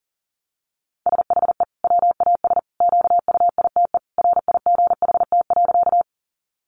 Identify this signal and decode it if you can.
Morse code, heard as a keyed tone, about 40 wpm, 700 Hz. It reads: SHE WAS QUITE RIGHT.
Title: Sentences from Top 700 Words 40wpm